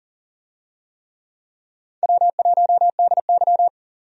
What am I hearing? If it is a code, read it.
Morse code, heard as a keyed tone, about 40 wpm, 700 Hz. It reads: W1DY
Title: USA Call Signs - 4 Characters - 40wpm